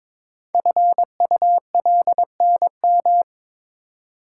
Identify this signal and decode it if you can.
Morse code, heard as a keyed tone, about 22 wpm, 700 Hz. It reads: FULNM
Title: Code Groups - 5 Letters - 22wpm